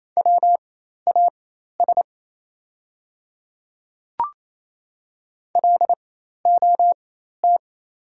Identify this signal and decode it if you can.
Morse code, heard as a keyed tone, about 28 wpm, spaced as for 13 wpm, 690 Hz. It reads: WAS LOT
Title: 4x Character Spacing - Top 500 Words - 28wpm